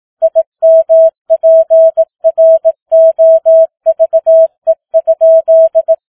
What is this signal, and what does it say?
Morse code, heard as a keyed tone, about 18 wpm, 650 Hz. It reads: IMPROVE?